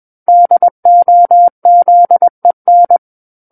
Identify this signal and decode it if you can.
Morse code, heard as a keyed tone, about 21 wpm, 700 Hz. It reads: DOZEN
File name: Book One / 12-Chapter 12 - 21 WPM